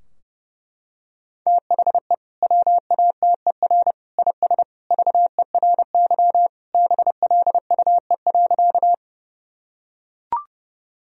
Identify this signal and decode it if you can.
Morse code, heard as a keyed tone, about 30 wpm, 700 Hz. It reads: THE WATER IS VERY BLUE.
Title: Sentences from Top 1000 Words - Review Part 3 - 30wpm